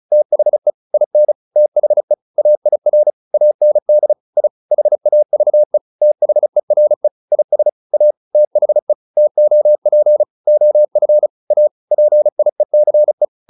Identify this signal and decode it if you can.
Morse code, heard as a keyed tone, about 34 wpm, 600 Hz. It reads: THE IN THE AIR AND I HAVE THERE IS A THE TOP OF A PIECE